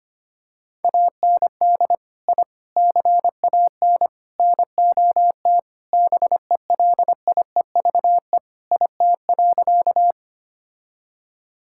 Morse code, heard as a keyed tone, about 25 wpm, 700 Hz. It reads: AND I CAN NOT BELIEVE IT.
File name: practice / 25wpm / Sentences with 4-Word Phrases - 25wpm